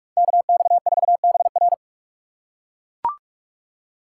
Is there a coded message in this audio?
Yes: Morse code from 0.2 to 1.7 s, heard as a keyed tone, about 44 wpm, 700 Hz. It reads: KX4BR